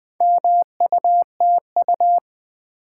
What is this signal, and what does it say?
Morse code, heard as a keyed tone, about 20 wpm, 700 Hz. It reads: MUTU
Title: Build A Word - 2 Letters At A Time - 6-Letter Words - 20wpm